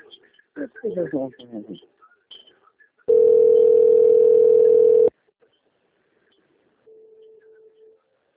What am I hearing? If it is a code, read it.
Ringback tone (North America), 440 + 480 Hz, with speech.
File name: out- - - - - .11595